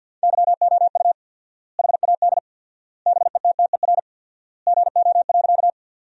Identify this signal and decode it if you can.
Morse code, heard as a keyed tone, about 50 wpm, 700 Hz. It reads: YOU HAD BETTER GO.
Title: 4x Word-Spacing - Sentences from Top 400 Words - 50wpm